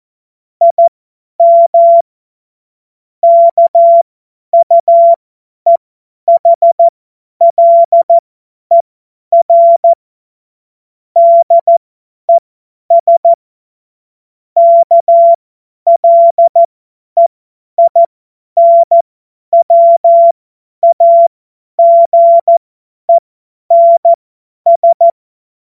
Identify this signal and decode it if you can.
Morse code, heard as a keyed tone, about 14 wpm, spaced as for 10 wpm, 680 Hz. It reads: IM KUEHLER DES KLEINWAGENS
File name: Morsekurs / Witze / Witz 01. März 2023 50 F70